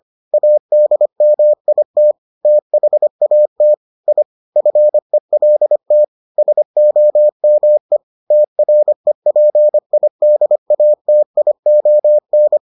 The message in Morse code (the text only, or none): ADMIT THAT I FELT SOME TREPIDATION